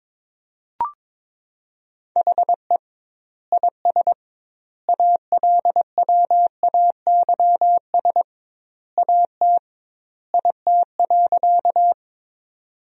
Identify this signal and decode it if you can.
Morse code, heard as a keyed tone, about 21 wpm, 700 Hz. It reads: HE IS ALWAYS AT IT.
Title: 2x Word-Spacing - Sentences from Top 300 Words - 22wpm